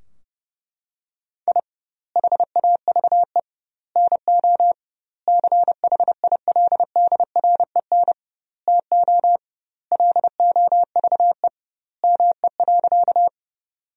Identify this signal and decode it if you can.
Morse code, heard as a keyed tone, about 32 wpm, 700 Hz. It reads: I HAVE NO CHILDREN TO LOVE ME.
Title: 2x Word-Spacing - Sentences from Top 400 Words - 30wpm